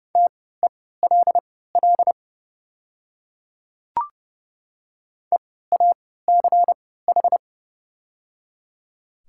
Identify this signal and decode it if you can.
Morse code, heard as a keyed tone, about 30 wpm, spaced as for 17 wpm, 700 Hz. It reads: TELL EACH
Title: 3x Character Spacing - Top 200 Words - 30wpm